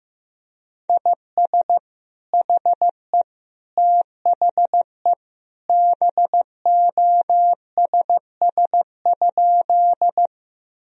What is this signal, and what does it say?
Morse code, heard as a keyed tone, about 16 wpm, 700 Hz. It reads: IS HE THE BOSS?
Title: Sentences from Top 1000 Words - Review Part 4 - 15wpm